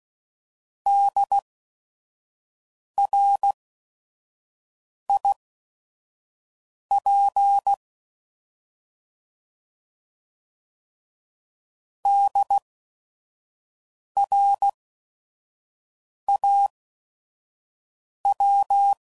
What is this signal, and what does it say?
Morse code, heard as a keyed tone, about 16 wpm, spaced as for 5 wpm, 800 Hz. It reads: DRIP DRAW
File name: Prefixes / By First Letter / Ds